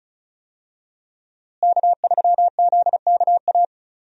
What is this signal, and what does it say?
Morse code, heard as a keyed tone, about 34 wpm, 700 Hz. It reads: K3ZKA
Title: USA Call Signs - 5 Characters - Repeated 2x - 35wpm